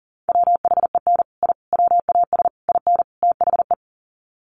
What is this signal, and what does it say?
Morse code, heard as a keyed tone, about 40 wpm, 700 Hz. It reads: WHEN I WAS IN THE
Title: Sets of 5 Words Encore2 40wpm